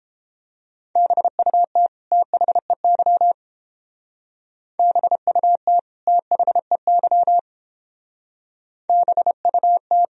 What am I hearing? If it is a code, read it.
Morse code, heard as a keyed tone, about 29 wpm, 700 Hz. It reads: BUT THEY BUT THEY BUT